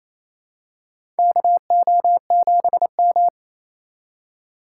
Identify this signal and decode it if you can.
Morse code, heard as a keyed tone, about 28 wpm, 700 Hz. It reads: KO7M
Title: USA Call Signs - 4 Characters - 4x Character Spacing - 28wpm